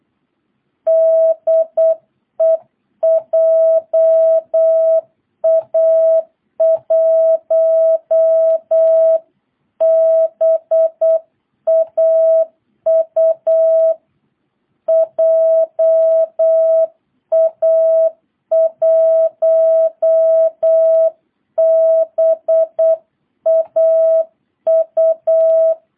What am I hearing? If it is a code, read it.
Morse code, heard as a keyed tone, about 8 wpm, 660 Hz. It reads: DE JA1BAU JA1BAU